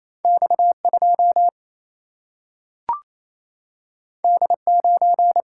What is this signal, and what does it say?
Morse code, heard as a keyed tone, about 28 wpm, 700 Hz. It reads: X2 D9
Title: Single-Letter Single-Number 28wpm